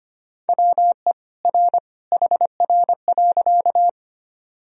Morse code, heard as a keyed tone, about 25 wpm, 700 Hz. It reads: WE R HR.
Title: Sentences from Top 200 Words and Abbreviations 25wpm